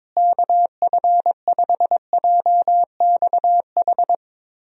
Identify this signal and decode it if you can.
Morse code, heard as a keyed tone, about 22 wpm, 700 Hz. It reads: KF5JXH
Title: USA Call Signs - 6 Characters - Repeated 2x - 22wpm